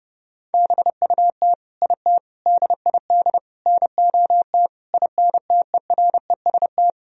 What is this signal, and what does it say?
Morse code, heard as a keyed tone, about 31 wpm, 700 Hz. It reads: BUT IT DID NOT INTEREST
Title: Sentences from Top 1000 Words - Review Part 2 - 30wpm